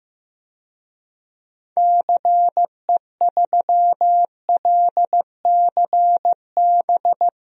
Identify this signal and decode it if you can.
Morse code, heard as a keyed tone, about 16 wpm, 700 Hz. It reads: CE3LCB
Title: DXCC-100 Call Signs - 6 Characters - 15wpm